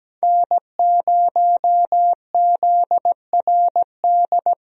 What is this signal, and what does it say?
Morse code, heard as a keyed tone, about 17 wpm, 700 Hz. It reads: N0ZRD
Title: USA Call Signs - 5 Characters - Repeated 2x - 17wpm